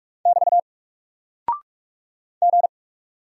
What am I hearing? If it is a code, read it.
Morse code, heard as a keyed tone, about 44 wpm, 700 Hz. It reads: <BT> G
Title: Single Character (Letter:Number:Punctuation:Prosign) 45wpm